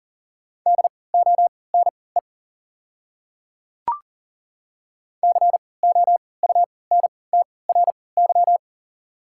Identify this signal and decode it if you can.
Morse code, heard as a keyed tone, about 40 wpm, spaced as for 23 wpm, 690 Hz. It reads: DONE COUNTRY
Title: 3x Character Spacing - Top 500 Words - 40wpm